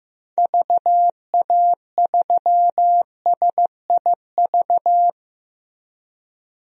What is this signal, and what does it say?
Morse code, heard as a keyed tone, about 15 wpm, 700 Hz. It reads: VA3SIV
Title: DXCC-100 Call Signs - 6 Characters - Repeated 2x - 15wpm